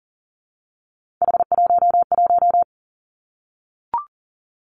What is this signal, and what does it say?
Morse code, heard as a keyed tone, about 42 wpm, 700 Hz. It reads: H11
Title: Three Challenging Letter-Number Combinations 40wpm